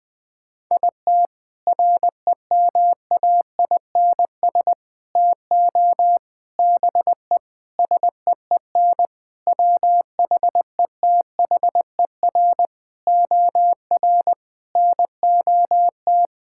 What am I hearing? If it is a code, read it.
Morse code, heard as a keyed tone, about 20 wpm, 700 Hz. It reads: IT REMAINS TO BE SEEN WHETHER OR NOT
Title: Sentences with 8-Word Phrases - Spoken Beginning - 20wpm